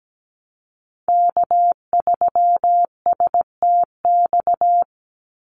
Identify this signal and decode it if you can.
Morse code, heard as a keyed tone, about 17 wpm, 700 Hz. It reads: K3STX
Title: Call Signs Repeated 3x - 17wpm